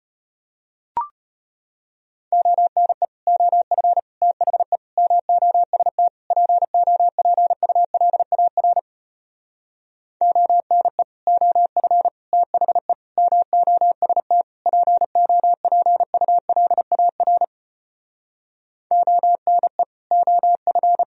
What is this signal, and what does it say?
Morse code, heard as a keyed tone, about 33 wpm, 700 Hz. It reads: ONE OF THE MOST POPULAR ONE OF THE MOST POPULAR ONE OF